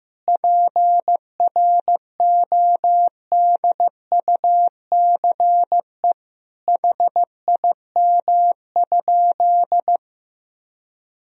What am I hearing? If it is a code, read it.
Morse code, heard as a keyed tone, about 15 wpm, 700 Hz. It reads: PRODUCE HIM?
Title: Sentences from Top 1000 Words - Review Part 3 - 15wpm